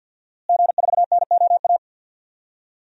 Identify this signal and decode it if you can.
Morse code, heard as a keyed tone, about 53 wpm, 700 Hz. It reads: G4NOA
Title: DXCC-100 Call Signs - 5 Characters - Repeated 3x - 50wpm